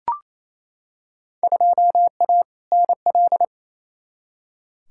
Morse code, heard as a keyed tone, about 27 wpm, 700 Hz. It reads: 2A NL
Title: Field Day Exchanges Encore - 28wpm